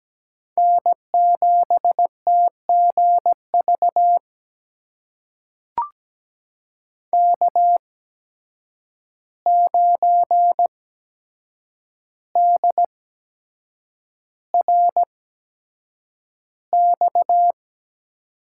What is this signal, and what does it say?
Morse code, heard as a keyed tone, about 17 wpm, spaced as for 7 wpm, 700 Hz. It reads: N7TGV K9DRX
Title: USA Call Signs - 5 Characters - 8x Character Spacing - 17wpm